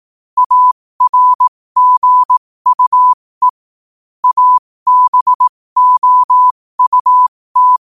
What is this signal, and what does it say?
Morse code, heard as a keyed tone, about 18 wpm, spaced as for 15 wpm, 1000 Hz. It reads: ARGUE ABOUT
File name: qotdfiles / 15wpm